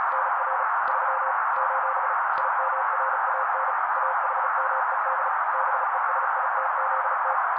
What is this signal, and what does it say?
Morse code, heard as a keyed tone, about 35 wpm, 530 Hz. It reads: NR 3 : IMRAN AHMED BESITZT
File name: daily / 35wpm noise